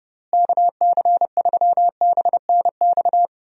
Morse code, heard as a keyed tone, about 30 wpm, 700 Hz. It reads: KC3BNX
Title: USA Call Signs - 6 Characters - Repeated 2x - 30wpm